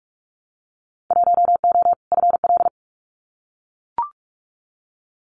Morse code, heard as a keyed tone, about 44 wpm, 700 Hz. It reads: 1O FL
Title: Winter Field Day Exchanges 45wpm